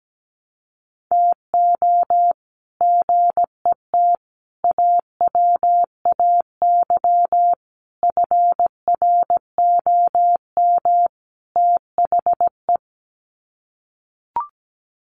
Morse code, heard as a keyed tone, about 17 wpm, 700 Hz. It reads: TO GET AWAY FROM THE
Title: Sets of 5 Words Encore2 17wpm